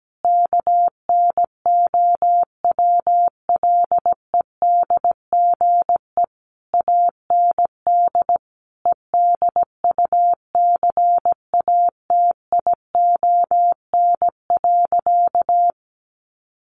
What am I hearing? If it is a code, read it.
Morse code, heard as a keyed tone, about 17 wpm, 700 Hz. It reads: KNOWLEDGE AND EDUCATION.
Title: Sentences from Top 1500 Words 17wpm